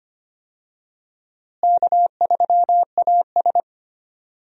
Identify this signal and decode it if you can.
Morse code, heard as a keyed tone, about 25 wpm, 700 Hz. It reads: K3AS
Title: USA Call Signs - 4 Characters - Repeated 2x - 25wpm